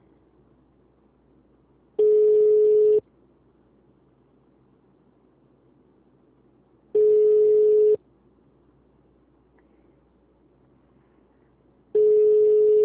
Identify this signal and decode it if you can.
Ringback tone (Europe, 425 Hz).
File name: out- -2006- - - .64